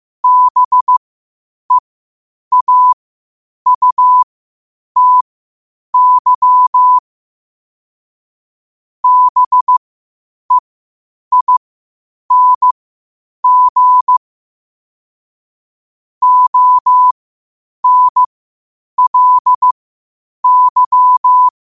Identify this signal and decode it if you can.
Morse code, heard as a keyed tone, about 15 wpm, spaced as for 8 wpm, 1000 Hz. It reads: BEAUTY BEING ONLY
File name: qotdfiles / 8wpm